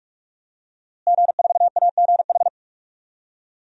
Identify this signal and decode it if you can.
Morse code, heard as a keyed tone, about 44 wpm, 700 Hz. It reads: G4AGH
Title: DXCC-100 Call Signs - 5 Characters - Repeated 3x - 45wpm